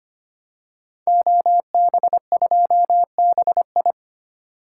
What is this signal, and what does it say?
Morse code, heard as a keyed tone, about 25 wpm, 700 Hz. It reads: OB2BI
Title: Code Groups - 5 Letter:Numbers - 25wpm